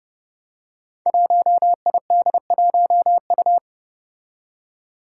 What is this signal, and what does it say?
Morse code, heard as a keyed tone, about 30 wpm, 700 Hz. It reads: 1ID1U
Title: Five Challenging Letter-Number Combinations 30wpm